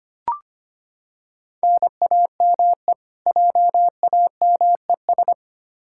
Morse code, heard as a keyed tone, about 25 wpm, 700 Hz. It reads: NAME JAMES
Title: Practice QSO - Exchange 1 - Beginning - 25wpm